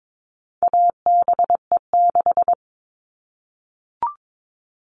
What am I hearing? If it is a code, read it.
Morse code, heard as a keyed tone, about 22 wpm, 700 Hz. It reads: ABE6
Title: All Cut Numbers - 4-Digit 22wpm